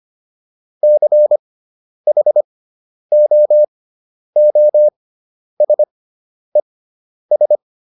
Morse code, heard as a keyed tone, about 25 wpm, spaced as for 10 wpm, 600 Hz. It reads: CHOOSES